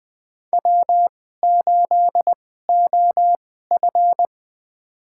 Morse code, heard as a keyed tone, about 20 wpm, spaced as for 14 wpm, 700 Hz. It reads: W8OF
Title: USA Call Signs - 2x Character Spacing - 20wpm